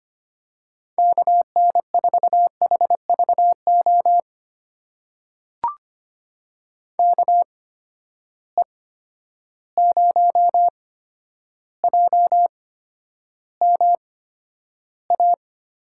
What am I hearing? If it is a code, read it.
Morse code, heard as a keyed tone, about 25 wpm, spaced as for 11 wpm, 700 Hz. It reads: KN4HVO KE0JMA
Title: USA Call Signs - 6 Characters - 8x Character Spacing - 25wpm